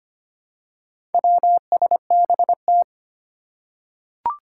Morse code, heard as a keyed tone, about 25 wpm, 700 Hz. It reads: WSBT
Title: Four Challenging Letter-Number Combinations 25wpm